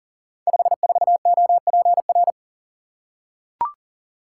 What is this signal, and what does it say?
Morse code, heard as a keyed tone, about 42 wpm, 700 Hz. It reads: 54OPR
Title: Code Groups - 5 Letter:Numbers - 40wpm